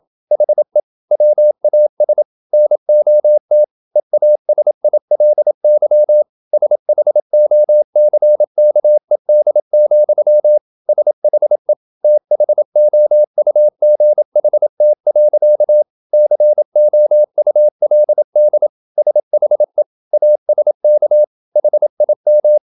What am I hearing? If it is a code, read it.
Morse code, heard as a keyed tone, about 27 wpm, 600 Hz. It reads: HE WAS NOT EASILY SHOCKED, SHE THOUGHT. COULD SHE ASK HIM
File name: Never 0027wpm